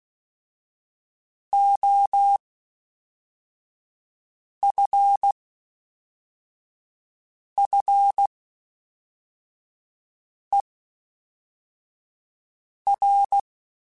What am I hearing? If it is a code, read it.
Morse code, heard as a keyed tone, about 16 wpm, spaced as for 4 wpm, 800 Hz. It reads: OFFER
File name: Special INGs / Special INGs-mp3s / Fast ing @10